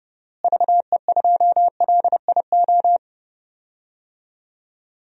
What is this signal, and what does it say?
Morse code, heard as a keyed tone, about 31 wpm, 700 Hz. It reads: VE2LIO
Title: DXCC-100 Call Signs - 6 Characters - Repeated 2x - 30wpm